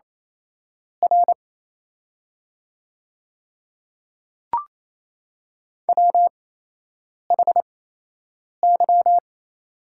Morse code, heard as a keyed tone, about 28 wpm, spaced as for 8 wpm, 700 Hz. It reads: R WHY
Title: 8x Character Spacing - Top 200 Words - 28wpm